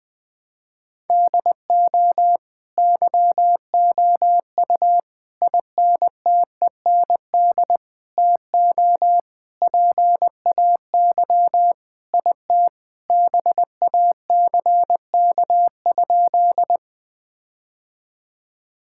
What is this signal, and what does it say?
Morse code, heard as a keyed tone, about 21 wpm, 700 Hz. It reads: DO YOU INTEND TO PAY IT BACK?
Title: Sentences from Top 1600 Words 20wpm